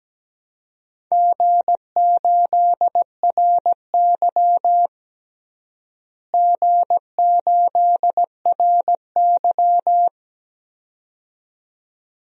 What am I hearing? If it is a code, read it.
Morse code, heard as a keyed tone, about 17 wpm, 700 Hz. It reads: G8RY G8RY
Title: DXCC-100 Call Signs - 4 Characters - Repeated 2x - 17wpm